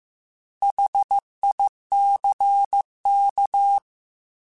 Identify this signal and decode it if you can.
Morse code, heard as a keyed tone, about 15 wpm, 800 Hz. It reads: HICK
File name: Alphabetical / Alphabetical mp3s / Hs @13